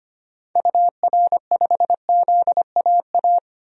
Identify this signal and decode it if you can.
Morse code, heard as a keyed tone, about 25 wpm, 700 Hz. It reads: UR5ZAA
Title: DXCC-100 Call Signs - 6 Characters - 25wpm